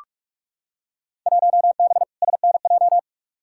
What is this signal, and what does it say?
Morse code, heard as a keyed tone, about 46 wpm, 700 Hz. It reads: 1B SNJ
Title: Field Day Exchanges Encore - 45wpm